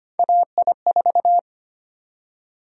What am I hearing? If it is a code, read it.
Morse code, heard as a keyed tone, about 25 wpm, 700 Hz. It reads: AI4
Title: USA Call Sign Prefixes - 25wpm